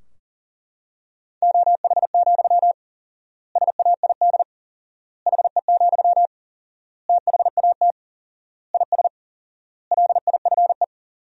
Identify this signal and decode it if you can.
Morse code, heard as a keyed tone, about 40 wpm, 700 Hz. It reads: OH, SAID HE, THAT IS LIFE
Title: 4x Word-Spacing - Sentences from Top 500 Words - 40wpm